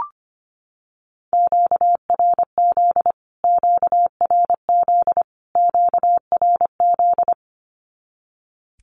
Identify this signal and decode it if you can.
Morse code, heard as a keyed tone, about 25 wpm, 700 Hz. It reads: QRZ QRZ QRZ
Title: Q-Codes Repeated 3x 25wpm